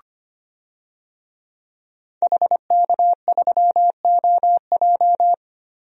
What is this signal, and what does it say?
Morse code, heard as a keyed tone, about 25 wpm, 700 Hz. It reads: HK3OJ
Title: DXCC-100 Call Signs - 5 Characters - 25wpm